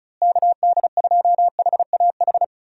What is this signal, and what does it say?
Morse code, heard as a keyed tone, about 34 wpm, 700 Hz. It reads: KD2HAH
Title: USA Call Signs - 6 Characters - Repeated 3x - 35wpm